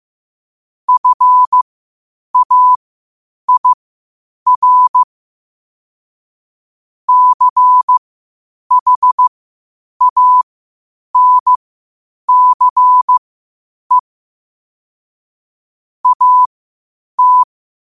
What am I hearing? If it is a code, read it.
Morse code, heard as a keyed tone, about 15 wpm, spaced as for 8 wpm, 1000 Hz. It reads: FAIR CHANCE AT